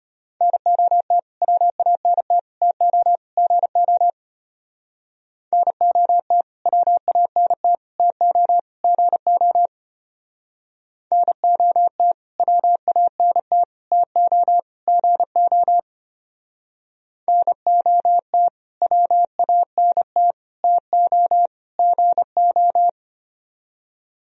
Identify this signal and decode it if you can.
Morse code, heard as a keyed tone, about 32 wpm, 700 Hz. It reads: NOT WANT TO GO NOT WANT TO GO NOT WANT TO GO NOT WANT TO GO